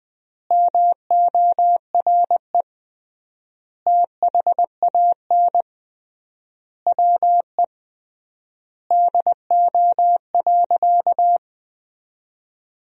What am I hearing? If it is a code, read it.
Morse code, heard as a keyed tone, about 20 wpm, 700 Hz. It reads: MORE THAN WE DO.
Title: 3x Word-Spacing - Sentences from Top 100 Words - 20wpm